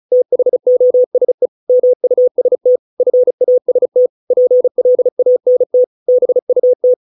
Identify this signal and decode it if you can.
Morse code, heard as a keyed tone, about 34 wpm, 500 Hz. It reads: THOSE MUST FAST PLANT BUT